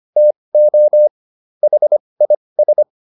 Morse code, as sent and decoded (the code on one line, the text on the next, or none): - --- / .... .. ...
TO HIS